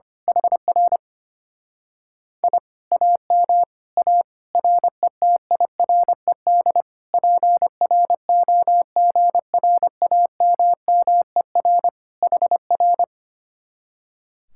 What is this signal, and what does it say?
Morse code, heard as a keyed tone, about 24 wpm, 700 Hz. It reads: HR I AM A RETIRED PROGRAMMER HR
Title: Speed-Racing Common QSO Phrases - 25wpm